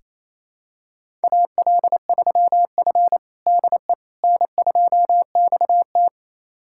Morse code, heard as a keyed tone, about 28 wpm, 700 Hz. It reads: AL3F DE N2XT
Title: USA Callsign DE Callsign - 4 Characters - 28wpm